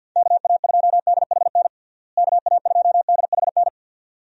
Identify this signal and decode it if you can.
Morse code, heard as a keyed tone, about 46 wpm, 700 Hz. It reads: KA2DHN KA2DHN
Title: USA Call Signs - 6 Characters - Repeated 2x - 50wpm